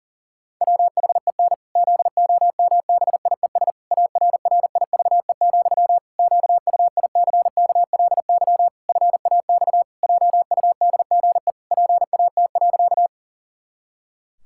Morse code, heard as a keyed tone, about 40 wpm, 700 Hz. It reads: WHEN ZOMBIES ARRIVE, QUICKLY FAX JUDGE PAT.
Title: Pangram Challenge v2 - 40wpm